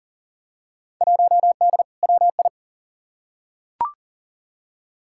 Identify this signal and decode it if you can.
Morse code, heard as a keyed tone, about 40 wpm, 700 Hz. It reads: 1D WI
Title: Field Day Exchanges - 40wpm